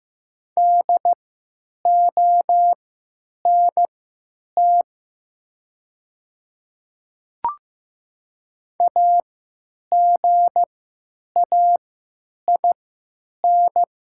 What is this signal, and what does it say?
Morse code, heard as a keyed tone, about 15 wpm, spaced as for 9 wpm, 700 Hz. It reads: DONT AGAIN
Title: 3x Character Spacing - Top 300 Words - 15wpm